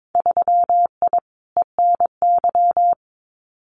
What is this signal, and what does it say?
Morse code, heard as a keyed tone, about 22 wpm, 700 Hz. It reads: 3I ENY
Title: Winter Field Day Exchanges Encore 22wpm